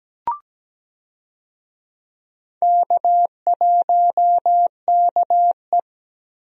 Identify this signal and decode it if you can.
Morse code, heard as a keyed tone, about 17 wpm, 700 Hz. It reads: K1KE
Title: USA Call Signs - 4 Characters - Repeated 3x - 17wpm